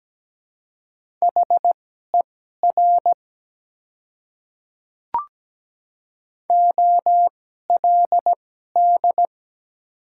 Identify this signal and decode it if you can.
Morse code, heard as a keyed tone, about 17 wpm, spaced as for 12 wpm, 700 Hz. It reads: HER OLD
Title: 2x Character Spacing - Top 300 Words - 17wpm